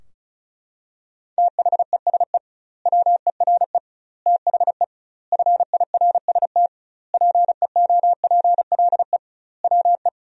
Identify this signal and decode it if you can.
Morse code, heard as a keyed tone, about 36 wpm, 700 Hz. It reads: THESE WERE THE FIRST PEOPLE WE
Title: 2x Word-Spacing - Sentences from Top 300 Words - 35wpm